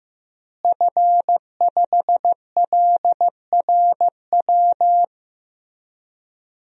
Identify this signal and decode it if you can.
Morse code, heard as a keyed tone, about 15 wpm, 700 Hz. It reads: F5LRW
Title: DXCC-100 Call Signs - 5 Characters - Repeated 2x - 15wpm